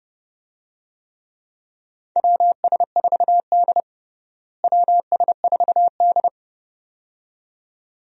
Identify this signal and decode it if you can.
Morse code, heard as a keyed tone, about 30 wpm, 700 Hz. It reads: WS4D WS4D
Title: USA Call Signs - 4 Characters - Repeated 2x - 30wpm